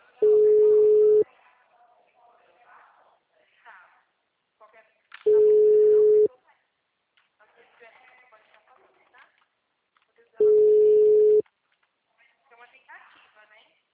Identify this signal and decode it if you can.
Ringback tone (Europe, 425 Hz), with speech.